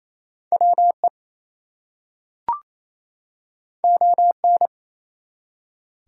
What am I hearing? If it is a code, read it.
Morse code, heard as a keyed tone, about 27 wpm, 700 Hz. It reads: WE ON
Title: Two Letter Words - 28wpm